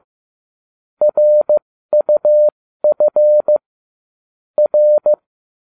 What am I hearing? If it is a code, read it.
Morse code, heard as a keyed tone, about 15 wpm, spaced as for 13 wpm, 600 Hz. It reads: RUF R